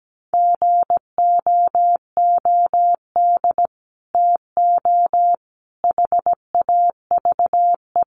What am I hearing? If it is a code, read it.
Morse code, heard as a keyed tone, about 17 wpm, 700 Hz. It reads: GOOD TO HAVE
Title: Sets of 5 Words Encore2 17wpm